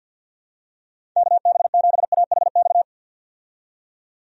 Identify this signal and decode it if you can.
Morse code, heard as a keyed tone, about 47 wpm, 700 Hz. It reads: KB7AHX
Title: USA Call Signs - 6 Characters - 50wpm